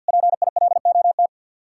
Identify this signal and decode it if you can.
Morse code, heard as a keyed tone, about 66 wpm, spaced as for 58 wpm, 700 Hz. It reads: PILOT
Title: Speed-Racing Common QSO Phrases - 50wpm